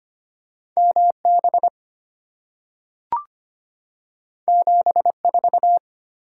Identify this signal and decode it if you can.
Morse code, heard as a keyed tone, about 25 wpm, 700 Hz. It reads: M B 7 4
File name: Two Letter-Number Combinations - 25wpm